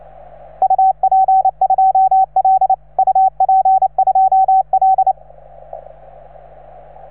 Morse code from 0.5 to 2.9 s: UP2L